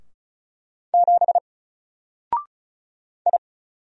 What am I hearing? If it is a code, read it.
Morse code, heard as a keyed tone, about 34 wpm, 700 Hz. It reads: 7 I
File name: Single-Letter-Number-35wpm